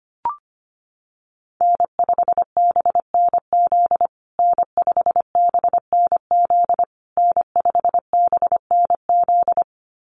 Morse code, heard as a keyed tone, about 25 wpm, 700 Hz. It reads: N5BNZ N5BNZ N5BNZ
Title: USA Call Signs Repeated 3x 25wpm